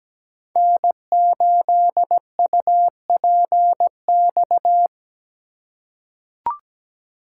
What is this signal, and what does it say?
Morse code, heard as a keyed tone, about 17 wpm, 700 Hz. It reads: N8UPX
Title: USA Call Signs - 5 Characters - 3x Character Spacing - 17wpm